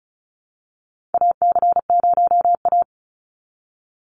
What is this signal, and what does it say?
Morse code, heard as a keyed tone, about 36 wpm, 700 Hz. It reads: AC0A
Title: NA SOTA Chaser Callsigns 35wpm